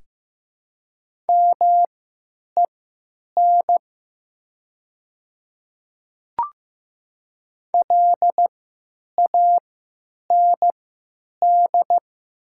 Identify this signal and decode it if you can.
Morse code, heard as a keyed tone, about 15 wpm, spaced as for 9 wpm, 700 Hz. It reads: MEN LAND
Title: 3x Character Spacing - Top 200 Words - 15wpm